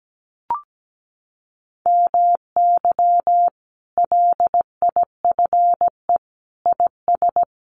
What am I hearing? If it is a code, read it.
Morse code, heard as a keyed tone, about 17 wpm, 700 Hz. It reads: MY LIFE IS
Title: Sentences from Top 1400 Words 17wpm